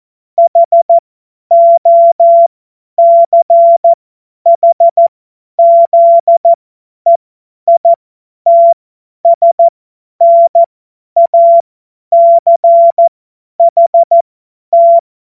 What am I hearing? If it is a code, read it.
Morse code, heard as a keyed tone, about 14 wpm, spaced as for 10 wpm, 680 Hz. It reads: HOCHZEITSNACHT
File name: Morsekurs / Witze / Witz 06. September 2022 50 F70